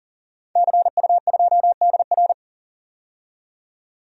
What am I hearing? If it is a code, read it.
Morse code, heard as a keyed tone, about 40 wpm, 700 Hz. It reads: CU2DR
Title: DXCC-100 Call Signs - 5 Characters - 40wpm